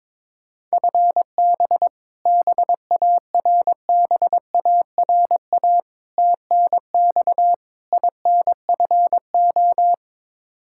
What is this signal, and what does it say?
Morse code, heard as a keyed tone, about 22 wpm, 700 Hz. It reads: FB BARBARA TNX INFO
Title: Practice QSO - Exchange 3 - More Info - 22wpm